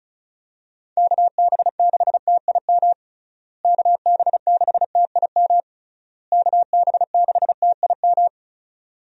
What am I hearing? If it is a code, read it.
Morse code, heard as a keyed tone, about 36 wpm, 700 Hz. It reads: KB6TIM KB6TIM KB6TIM
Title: USA Call Signs - 6 Characters - Repeated 3x - 35wpm